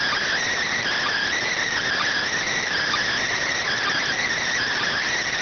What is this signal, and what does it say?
Emergency Alert System (SAME) tones, heard as harsh buzzing bursts. It is heard from 1.1 to 4.3 s.